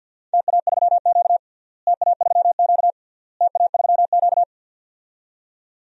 Morse code, heard as a keyed tone, about 50 wpm, 700 Hz. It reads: TA3Q TA3Q TA3Q
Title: DXCC-100 Call Signs - 4 Characters - Repeated 3x - 50wpm